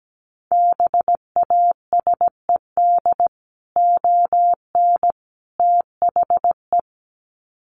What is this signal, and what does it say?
Morse code, heard as a keyed tone, about 17 wpm, 700 Hz. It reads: BASED ON THE
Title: Sets of 3 Words 17wpm